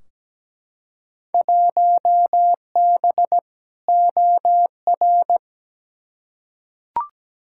Morse code, heard as a keyed tone, about 17 wpm, 700 Hz. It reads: 1B OR
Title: Field Day Exchanges - 17wpm